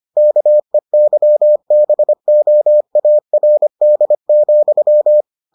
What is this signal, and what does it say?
Morse code, heard as a keyed tone, about 25 wpm, 600 Hz. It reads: KEYBOARD,